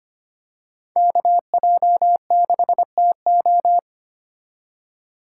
Morse code, heard as a keyed tone, about 25 wpm, 700 Hz. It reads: KJ6TO
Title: USA Call Signs - 5 Characters - 2x Character Spacing - 25wpm